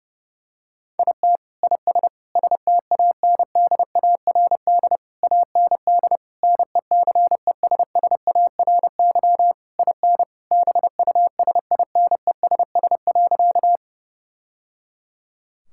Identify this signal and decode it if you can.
Morse code, heard as a keyed tone, about 32 wpm, 700 Hz. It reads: IT IS STANDARD AND NECESSARY IN BUSINESS.
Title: Sentences from Top 1000 Words - Review Part 6 - 30wpm